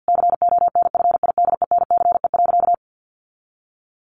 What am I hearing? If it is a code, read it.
Morse code, heard as a keyed tone, about 48 wpm, 700 Hz. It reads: CONFIDENCE.
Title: Sentences from Top 1500 Words 50wpm